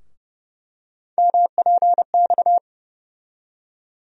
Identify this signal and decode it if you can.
Morse code, heard as a keyed tone, about 30 wpm, 700 Hz. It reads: MPX
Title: USA Call Sign Postfixes - 3 Letters 30wpm